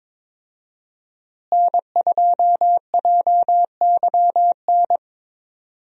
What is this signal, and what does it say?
Morse code, heard as a keyed tone, about 22 wpm, 700 Hz. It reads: N2JYN
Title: USA Call Signs - 5 Characters - Repeated 3x - 22wpm